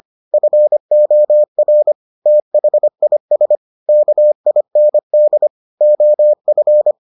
Morse code, heard as a keyed tone, about 25 wpm, 600 Hz. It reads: FOR THIS KIND OF